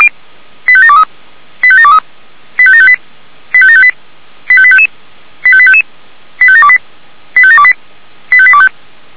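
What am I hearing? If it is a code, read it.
Five-tone selective call (ZVEI), heard as quick runs of five tones, tones of about 70 ms. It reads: 85723 85723 85758 85758 85855 85855 85728 85728 85825
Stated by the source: ZVEI five-tone selective call 85759, received with an SDR.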